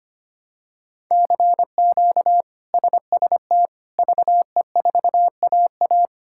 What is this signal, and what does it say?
Morse code, heard as a keyed tone, about 25 wpm, 700 Hz. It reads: CQ SST VE4AA
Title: K1USN SST QSO Practice - 25wpm